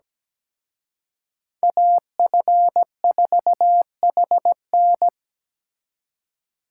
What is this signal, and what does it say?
Morse code, heard as a keyed tone, about 20 wpm, 700 Hz. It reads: AF4HN
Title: Speed Racing - USA 2x2 Call Signs 17wpm